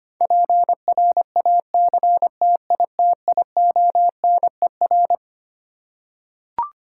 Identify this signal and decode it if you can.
Morse code, heard as a keyed tone, about 25 wpm, 700 Hz. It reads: PRACTITIONER
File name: Twelve Letter Words - 25wpm